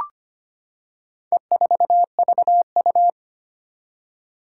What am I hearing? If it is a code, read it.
Morse code, heard as a keyed tone, about 25 wpm, 700 Hz. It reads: E4VU
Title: Four Challenging Letter-Number Combinations 25wpm